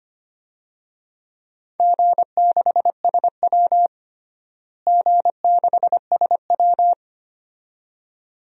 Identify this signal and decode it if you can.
Morse code, heard as a keyed tone, about 25 wpm, 700 Hz. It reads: G6SW G6SW
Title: DXCC-100 Call Signs - 4 Characters - Repeated 2x - 25wpm